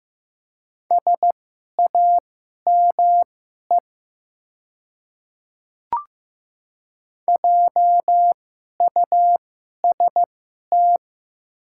Morse code, heard as a keyed tone, about 15 wpm, spaced as for 11 wpm, 700 Hz. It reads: SAME JUST
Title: 2x Character Spacing - Top 200 Words - 15wpm